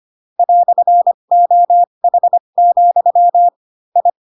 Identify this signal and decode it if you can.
Morse code, heard as a keyed tone, about 25 wpm, spaced as for 22 wpm, 700 Hz. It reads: "OH, I